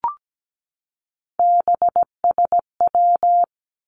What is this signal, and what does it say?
Morse code, heard as a keyed tone, about 17 wpm, 700 Hz. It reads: BSW
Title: Three Challenging Letter-Number Combinations 17wpm